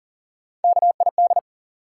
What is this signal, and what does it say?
Morse code, heard as a keyed tone, about 40 wpm, 700 Hz. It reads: KID
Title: Sets of 5 Words - New Word Drill - 40wpm